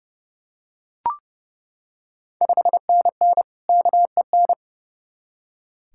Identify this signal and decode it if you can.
Morse code, heard as a keyed tone, about 32 wpm, 700 Hz. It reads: 5NN KEN
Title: Ontario QSO Party Exchange 30wpm